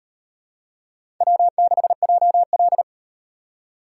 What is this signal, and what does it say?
Morse code, heard as a keyed tone, about 38 wpm, 700 Hz. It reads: W6JL